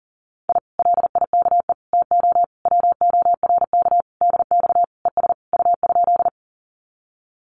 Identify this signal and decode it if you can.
Morse code, heard as a keyed tone, about 40 wpm, 700 Hz. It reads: I LIKE TO WORK DX ES U?
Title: Common QSO Phrases Encore 40wpm